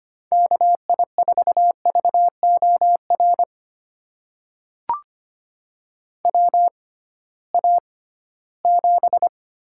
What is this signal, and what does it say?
Morse code, heard as a keyed tone, about 25 wpm, spaced as for 16 wpm, 700 Hz. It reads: KI4VOR WA7
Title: USA Call Signs - 6 Characters - 6x Character Spacing - 25wpm